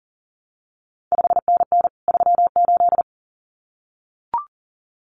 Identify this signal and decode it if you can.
Morse code, heard as a keyed tone, about 38 wpm, 700 Hz. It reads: 5NN 38
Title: CQ WW DX Contest Exchange Encore - 40wpm